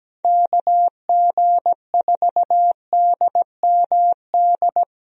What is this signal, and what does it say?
Morse code, heard as a keyed tone, about 17 wpm, 700 Hz. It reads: KG4DMD
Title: USA Call Signs - 6 Characters - Repeated 3x - 17wpm